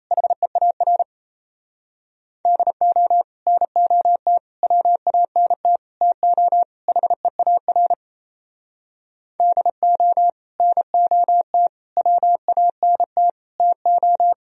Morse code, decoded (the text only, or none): HEAR DO NOT WANT TO HEAR DO NOT WANT TO